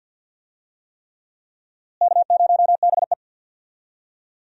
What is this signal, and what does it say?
Morse code, heard as a keyed tone, about 53 wpm, 700 Hz. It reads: K0BE
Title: USA Call Signs - 4 Characters - Repeated 3x - 50wpm